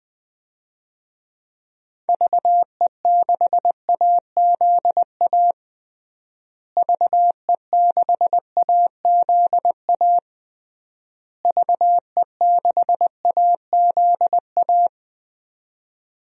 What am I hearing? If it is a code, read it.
Morse code, heard as a keyed tone, about 20 wpm, 700 Hz. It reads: VE6AZA VE6AZA VE6AZA
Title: DXCC-100 Call Signs - 6 Characters - Repeated 3x - 20wpm